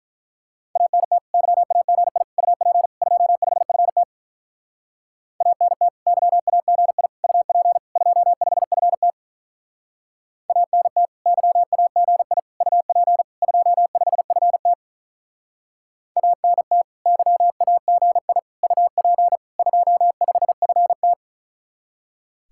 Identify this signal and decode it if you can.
Morse code, heard as a keyed tone, about 43 wpm, 700 Hz. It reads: ANT YAGI UP 25FT ANT YAGI UP 25FT ANT YAGI UP 25FT ANT YAGI UP 25FT
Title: Speed-Racing Common QSO Phrases Encore - 35wpm